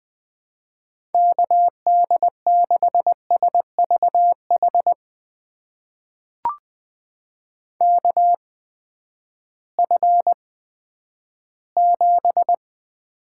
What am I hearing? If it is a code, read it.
Morse code, heard as a keyed tone, about 20 wpm, spaced as for 11 wpm, 700 Hz. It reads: KD6SVH KF7
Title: USA Call Signs - 6 Characters - 8x Character Spacing - 20wpm